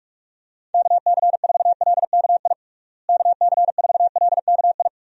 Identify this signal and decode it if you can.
Morse code, heard as a keyed tone, about 46 wpm, 700 Hz. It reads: KC4LKI KC4LKI
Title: USA Call Signs - 6 Characters - Repeated 3x - 45wpm